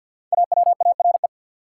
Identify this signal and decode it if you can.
Morse code, heard as a keyed tone, about 53 wpm, 700 Hz. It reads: AWARE
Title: Sentences from Top 1200 Words - New Word Drill - 50wpm